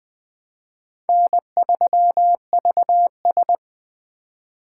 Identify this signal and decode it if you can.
Morse code, heard as a keyed tone, about 20 wpm, 700 Hz. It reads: N3VS
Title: USA Call Signs - 4 Characters - 4x Character Spacing - 20wpm